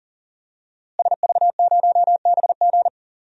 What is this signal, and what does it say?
Morse code, heard as a keyed tone, about 40 wpm, 700 Hz. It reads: SV0BG
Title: DXCC-100 Call Signs - 5 Characters - Repeated 3x - 40wpm